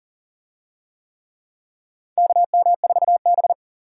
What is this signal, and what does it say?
Morse code, heard as a keyed tone, about 40 wpm, 700 Hz. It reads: KM4B
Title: USA Call Signs - 4 Characters - Repeated 3x - 40wpm